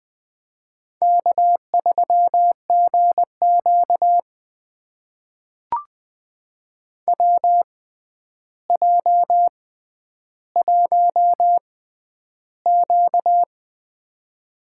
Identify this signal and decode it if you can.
Morse code, heard as a keyed tone, about 20 wpm, spaced as for 10 wpm, 700 Hz. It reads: K3GQ WJ1Q
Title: USA Call Signs - 4 Characters - 6x Character Spacing - 20wpm